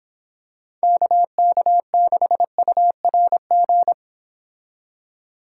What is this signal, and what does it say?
Morse code, heard as a keyed tone, about 26 wpm, 700 Hz. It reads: KK6URG